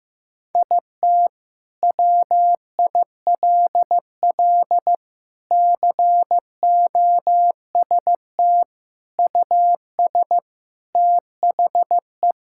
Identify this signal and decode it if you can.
Morse code, heard as a keyed tone, about 15 wpm, 700 Hz. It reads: IT WILL COST US THE
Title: Sentences from Top 1000 Words - Review Part 5 - 15wpm